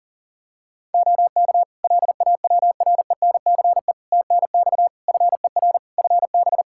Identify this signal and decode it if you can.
Morse code, heard as a keyed tone, about 37 wpm, 700 Hz. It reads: OK LAWRENCE TNX FER FB
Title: Practice QSO - Exchange 4 - Ending - 40wpm